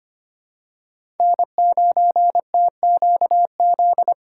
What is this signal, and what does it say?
Morse code, heard as a keyed tone, about 25 wpm, 700 Hz. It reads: N9TQZ
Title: USA Call Signs - 5 Characters - 6x Character Spacing - 25wpm